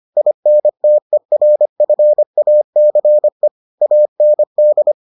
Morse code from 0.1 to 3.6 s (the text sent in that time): INTERFACE